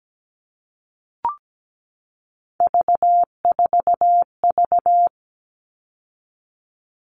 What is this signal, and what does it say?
Morse code, heard as a keyed tone, about 17 wpm, 700 Hz. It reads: V4V
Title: All Cut Numbers - 3-Digit 17wpm